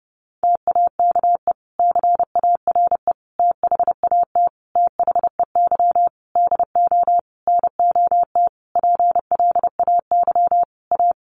Morse code, heard as a keyed tone, about 30 wpm, 700 Hz. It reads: TAKE CARE THAT THEY DO NOT PLAY A
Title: Sentences from Top 1500 Words 30wpm